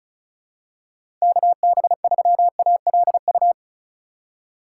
Morse code, heard as a keyed tone, about 34 wpm, 700 Hz. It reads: KB3ALU
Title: USA Call Signs - 6 Characters - 4x Character Spacing - 35wpm